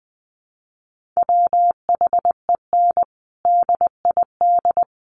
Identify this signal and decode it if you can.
Morse code, heard as a keyed tone, about 20 wpm, 700 Hz. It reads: WHEN DID
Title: Sentences from Top 300 Words 20wpm